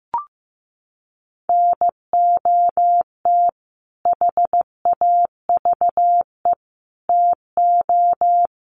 Morse code, heard as a keyed tone, about 15 wpm, 700 Hz. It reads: NOT HAVE TO
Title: Sets of 5 Words Encore 15wpm